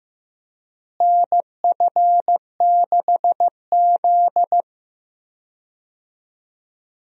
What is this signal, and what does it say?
Morse code, heard as a keyed tone, about 15 wpm, 700 Hz. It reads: NF6Z
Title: USA Call Signs - 4 Characters - Repeated 3x - 15wpm